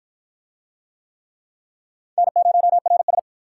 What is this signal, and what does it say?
Morse code, heard as a keyed tone, about 53 wpm, 700 Hz. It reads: N0RS